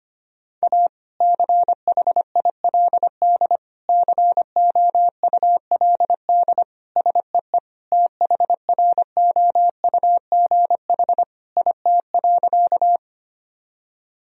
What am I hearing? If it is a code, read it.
Morse code, heard as a keyed tone, about 25 wpm, 700 Hz. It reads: A CHILD COULD SEE THROUGH IT.
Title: Sentences from Top 1000 Words - Review Part 1 - 25wpm